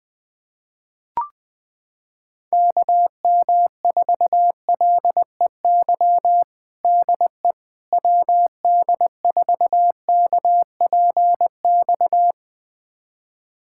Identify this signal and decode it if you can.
Morse code, heard as a keyed tone, about 21 wpm, 700 Hz. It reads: KM4LEY DE WD4KPX
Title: USA Callsign DE Callsign 20wpm